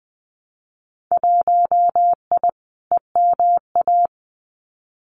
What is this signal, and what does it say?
Morse code, heard as a keyed tone, about 20 wpm, 700 Hz. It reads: 1I EMA
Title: Winter Field Day Exchanges Encore 20wpm